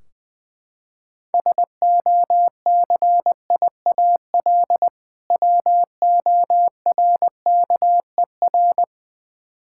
Morse code, heard as a keyed tone, about 21 wpm, 700 Hz. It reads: SOCIAL WORKER
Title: Practice QSO - Exchange 3 - More Info - 20wpm